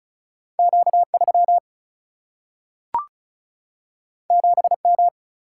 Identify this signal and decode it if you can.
Morse code, heard as a keyed tone, about 36 wpm, 700 Hz. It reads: Q3 7M
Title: Single-Letter Single-Number 35wpm